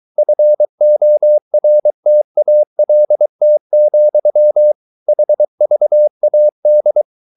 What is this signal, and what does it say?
Morse code, heard as a keyed tone, about 23 wpm, 600 Hz. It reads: FORTALT, HVAD
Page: Ferie 23 wpm